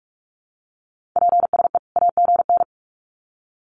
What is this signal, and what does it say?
Morse code, heard as a keyed tone, about 44 wpm, 700 Hz. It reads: PSE AGN
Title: Common QSO Phrases 45wpm